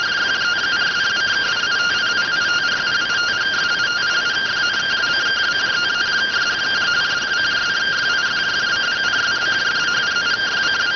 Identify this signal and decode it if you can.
Two-tone teleprinter signal, about 170 Hz shift.